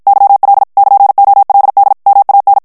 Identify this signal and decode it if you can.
Morse code, heard as a keyed tone, about 54 wpm, 800 Hz. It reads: YL <KN>OLD MAN
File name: ITALIEN texte clair 1 vitesse 50 mots minute